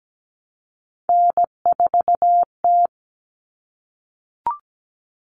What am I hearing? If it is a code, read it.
Morse code, heard as a keyed tone, about 17 wpm, 700 Hz. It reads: N4T
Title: All Cut Numbers - 3-Digit 17wpm